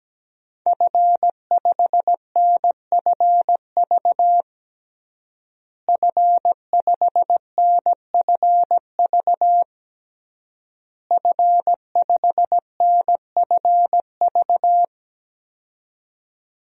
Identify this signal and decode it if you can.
Morse code, heard as a keyed tone, about 17 wpm, 700 Hz. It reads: F5NFV F5NFV F5NFV
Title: DXCC-100 Call Signs - 5 Characters - Repeated 3x - 17wpm